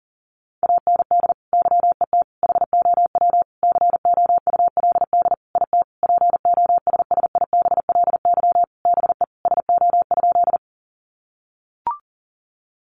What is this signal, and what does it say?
Morse code, heard as a keyed tone, about 40 wpm, 700 Hz. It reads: AND YET HOW COULD IT POSSIBLY BE SO?
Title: Sentences from Top 1300 Words 40wpm